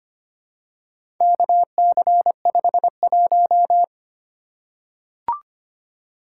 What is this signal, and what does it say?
Morse code, heard as a keyed tone, about 25 wpm, 700 Hz. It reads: KC51
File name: practice / 25wpm / Four Letter-Number Combinations - 25wpm